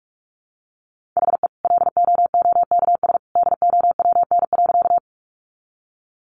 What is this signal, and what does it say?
Morse code, heard as a keyed tone, about 44 wpm, 700 Hz. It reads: HE LOOKS DOWN.
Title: Additional Easy Phrases 45wpm